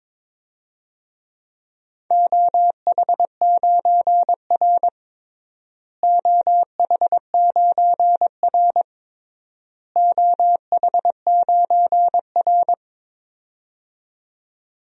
Morse code, heard as a keyed tone, about 22 wpm, 700 Hz. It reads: OH9R OH9R OH9R
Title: DXCC-100 Call Signs - 4 Characters - Repeated 3x - 22wpm